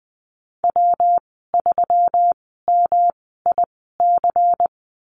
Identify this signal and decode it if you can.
Morse code, heard as a keyed tone, about 19 wpm, spaced as for 14 wpm, 700 Hz. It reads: W3MIC
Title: 219 Lesson 35g - Number 0 - Call Signs 20wpm